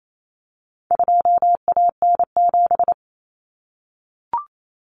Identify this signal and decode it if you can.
Morse code, heard as a keyed tone, about 28 wpm, 700 Hz. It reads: 2AN7
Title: Common Cut Numbers - 4-Digit 28wpm